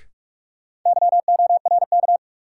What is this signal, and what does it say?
Morse code, heard as a keyed tone, about 45 wpm, 700 Hz. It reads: YORK
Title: Sets of 2 Words - New Word Drill - 45wpm